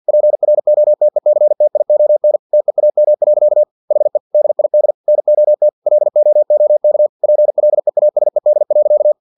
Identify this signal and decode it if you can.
Morse code, heard as a keyed tone, about 50 wpm, 600 Hz. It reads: PROTECTION TEAM. HE DID NOT LOOK PLEASED.
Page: Never 49wpm